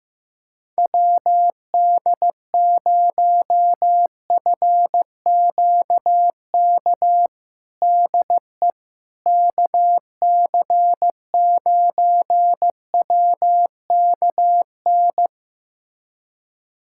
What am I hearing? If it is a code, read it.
Morse code, heard as a keyed tone, about 15 wpm, 700 Hz. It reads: WD0FQK DE KC9WKN
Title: USA Callsign DE Callsign 15wpm